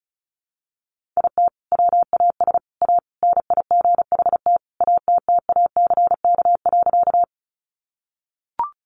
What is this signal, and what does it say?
Morse code, heard as a keyed tone, about 34 wpm, 700 Hz. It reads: IT WAS A NIGHT ATTACK.
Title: Sentences from Top 1500 Words 35wpm